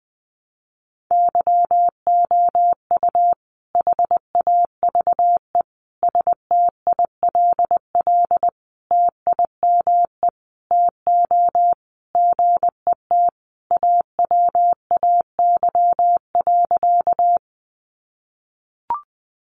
Morse code, heard as a keyed tone, about 20 wpm, 700 Hz. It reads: YOU HAVE STILL TIME TO GET AWAY.
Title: Sentences from Top 600 Words 20wpm